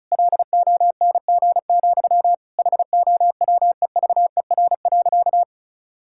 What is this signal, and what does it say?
Morse code, heard as a keyed tone, about 34 wpm, 700 Hz. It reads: LONG, HOWEVER.
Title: Sentences from Top 1000 Words - Review Part 6 - 35wpm